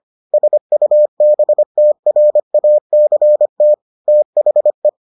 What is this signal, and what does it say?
Morse code, heard as a keyed tone, about 25 wpm, 600 Hz. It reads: SUBTRACT THE